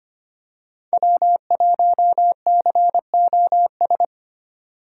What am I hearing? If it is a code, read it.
Morse code, heard as a keyed tone, about 25 wpm, 700 Hz. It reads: W1COS
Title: USA Call Signs - 5 Characters - 4x Character Spacing - 25wpm